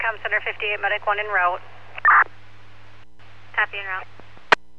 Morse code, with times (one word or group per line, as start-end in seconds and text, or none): none